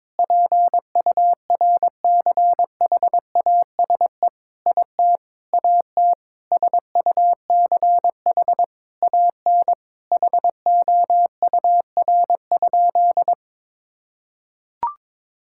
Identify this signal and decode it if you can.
Morse code, heard as a keyed tone, about 22 wpm, 700 Hz. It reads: PURCHASE IT AT SUCH AN HOUR?
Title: Sentences from Top 1000 Words - Review Part 5 - 22wpm